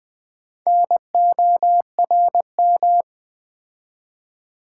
Morse code, heard as a keyed tone, about 20 wpm, 700 Hz. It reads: NORM
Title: Build A Word - 2 Letters At A Time - 6-Letter Words - 20wpm